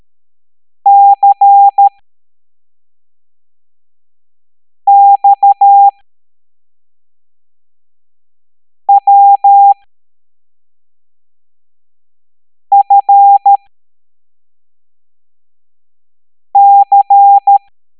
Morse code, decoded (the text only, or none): CXWFC